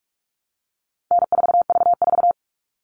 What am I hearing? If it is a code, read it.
Morse code, heard as a keyed tone, about 44 wpm, 700 Hz. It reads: N4V4
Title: All Cut Numbers - 4-Digit 45wpm